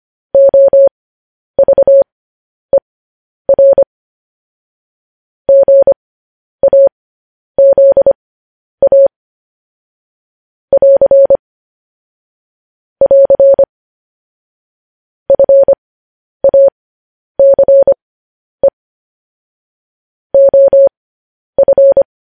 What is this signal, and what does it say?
Morse code, heard as a keyed tone, about 25 wpm, spaced as for 10 wpm, 550 Hz. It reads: OVER GAZA <AR> <AR> FACE OF